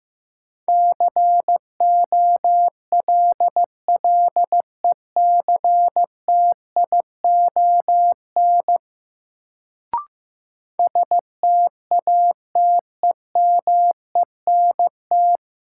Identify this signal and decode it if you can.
Morse code, heard as a keyed tone, about 15 wpm, 700 Hz. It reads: COLLECTION STATEMENT
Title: Sentences from Top 1200 Words - New Word Drill Alt - 15wpm